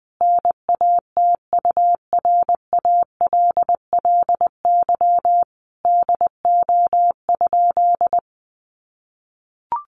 Morse code, heard as a keyed tone, about 20 wpm, 700 Hz. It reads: NATURALLY DO?
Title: Sentences from Top 1300 Words 20wpm